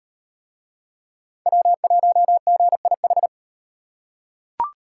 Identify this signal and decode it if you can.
Morse code, heard as a keyed tone, about 37 wpm, 700 Hz. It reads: W1GIH